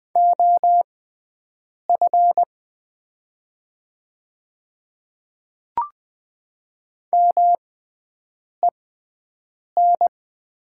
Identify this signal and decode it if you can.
Morse code, heard as a keyed tone, about 20 wpm, spaced as for 7 wpm, 700 Hz. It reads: OF MEN
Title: 6x Character Spacing - Top 200 Words - 20wpm